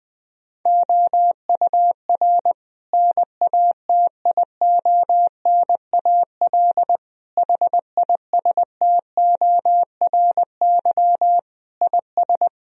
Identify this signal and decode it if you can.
Morse code, heard as a keyed tone, about 20 wpm, 700 Hz. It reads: OUR NATIONAL HISTORY IS
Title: Sentences from Top 1000 Words - Review Part 4 - 20wpm